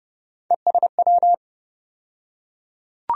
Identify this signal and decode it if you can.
Morse code, heard as a keyed tone, about 30 wpm, 700 Hz. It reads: ESW
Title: USA Call Sign Postfixes - 3 Letters 30wpm